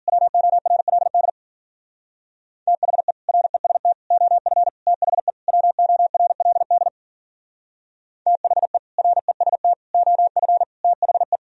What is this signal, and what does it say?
Morse code, heard as a keyed tone, about 53 wpm, 700 Hz. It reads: WORLD THE REST OF THE WORLD THE REST OF THE